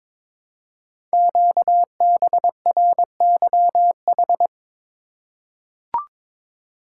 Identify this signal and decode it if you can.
Morse code, heard as a keyed tone, about 22 wpm, 700 Hz. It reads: QBRYH
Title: Code Groups - 5 Letters - 22wpm